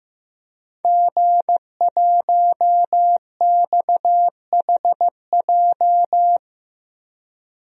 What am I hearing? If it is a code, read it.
Morse code, heard as a keyed tone, about 15 wpm, 700 Hz. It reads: G1XHJ
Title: DXCC-100 Call Signs - 5 Characters - Repeated 3x - 15wpm